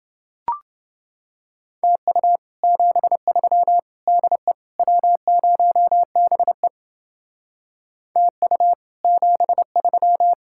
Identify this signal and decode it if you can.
Morse code, heard as a keyed tone, about 29 wpm, 700 Hz. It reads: TU 73 DE W0BE TU 73